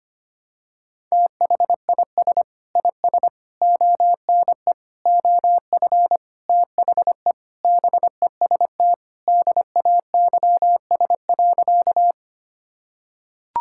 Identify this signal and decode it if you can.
Morse code, heard as a keyed tone, about 25 wpm, 700 Hz. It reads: THIS IS ONE OF THE BEST DAYS.
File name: practice / 25wpm / Sentences with 4-Word Phrases - 25wpm